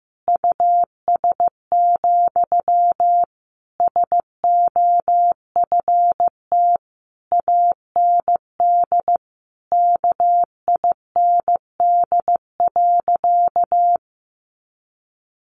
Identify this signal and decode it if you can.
Morse code, heard as a keyed tone, about 15 wpm, 700 Hz. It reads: US, SOFT AND KIND.
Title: Sentences from Top 1500 Words 15wpm